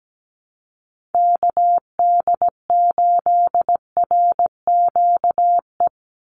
Morse code, heard as a keyed tone, about 17 wpm, 700 Hz. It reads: KD8RQE
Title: Call Signs Repeated 2x - 17wpm